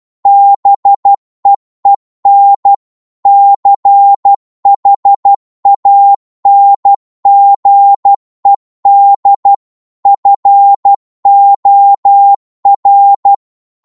Morse code, heard as a keyed tone, about 12 wpm, 800 Hz. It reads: BEEN CHANGED FOR